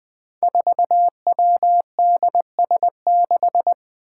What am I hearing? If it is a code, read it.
Morse code, heard as a keyed tone, about 20 wpm, 700 Hz. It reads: 4WDS6
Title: Five Challenging Letter-Number Combinations 20wpm